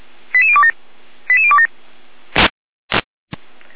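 Five-tone selective call (ZVEI), heard as quick runs of five tones, tones of about 70 ms. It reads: 88027 88027
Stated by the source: ZVEI five-tone selective call received on an SDR.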